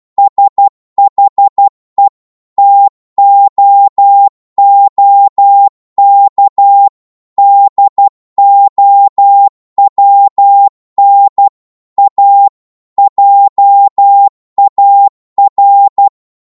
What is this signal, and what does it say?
Morse code, heard as a keyed tone, about 12 wpm, 800 Hz. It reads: SHE TOOK DOWN A JAR
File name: cwmp3 / psg-1-12wpm